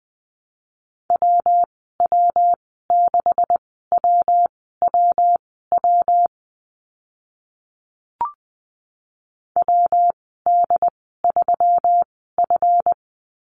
Morse code, heard as a keyed tone, about 20 wpm, spaced as for 15 wpm, 700 Hz. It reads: WW6WWW WD3F
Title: 140 Lesson 23g - Letter f - Call Signs 20wpm